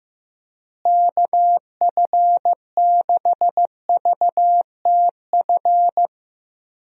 Morse code, heard as a keyed tone, about 15 wpm, 700 Hz. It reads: KF6VTF
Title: USA Call Signs - 6 Characters - Repeated 3x - 15wpm